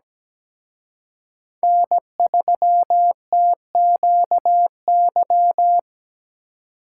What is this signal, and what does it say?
Morse code, heard as a keyed tone, about 20 wpm, 700 Hz. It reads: N3TQY
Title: Speed Racing - USA 1x3 Call Signs 17wpm